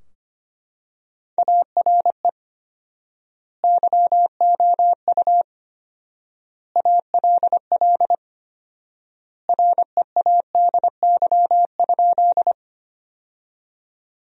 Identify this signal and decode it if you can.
Morse code, heard as a keyed tone, about 25 wpm, 700 Hz. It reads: ARE YOU ALL READY?
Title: 4x Word-Spacing - Sentences from Top 400 Words - 25wpm